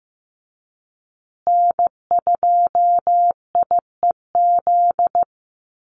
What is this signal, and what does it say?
Morse code, heard as a keyed tone, about 14 wpm, 700 Hz. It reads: N2IEZ
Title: USA Call Signs Repeated 3x 15wpm